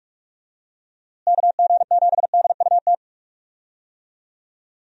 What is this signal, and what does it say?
Morse code, heard as a keyed tone, about 47 wpm, 700 Hz. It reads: KG7DUT
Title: USA Call Signs - 6 Characters - Repeated 3x - 45wpm